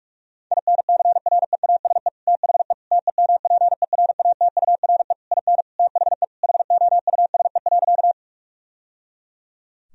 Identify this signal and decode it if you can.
Morse code, heard as a keyed tone, about 48 wpm, 700 Hz. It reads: INCREASE THE TEMPERATURE IN THE HOUSE.
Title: Sentences from Top 1000 Words - Review Part 4 - 45wpm